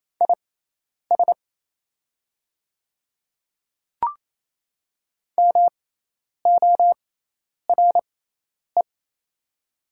Morse code, heard as a keyed tone, about 28 wpm, spaced as for 10 wpm, 700 Hz. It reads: IS MORE
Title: 6x Character Spacing - Top 100 Words - 28wpm